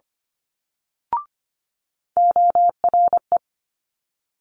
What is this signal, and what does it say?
Morse code, heard as a keyed tone, about 25 wpm, 700 Hz. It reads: ORE
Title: Multi-letter Phonemes Quick Fire 25wpm